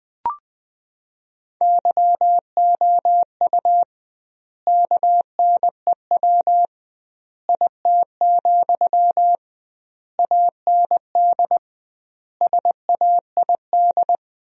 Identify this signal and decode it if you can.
Morse code, heard as a keyed tone, about 20 wpm, 700 Hz. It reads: YOU KNEW IT, AND SAID
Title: 2x Word-Spacing - Sentences from Top 500 Words - 20wpm